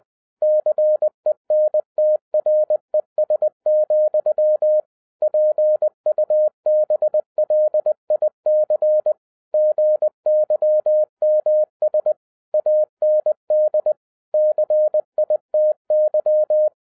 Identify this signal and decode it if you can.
Morse code, heard as a keyed tone, about 20 wpm, 610 Hz. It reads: CENTRES, PUBLIC GYMS AND CITY
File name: 2011-07-mp3 / n33 THE 3.11 2 20wpm